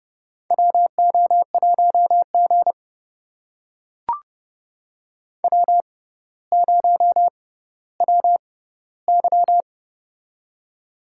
Morse code, heard as a keyed tone, about 30 wpm, spaced as for 15 wpm, 700 Hz. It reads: WO1G W0WY
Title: USA Call Signs - 4 Characters - 6x Character Spacing - 30wpm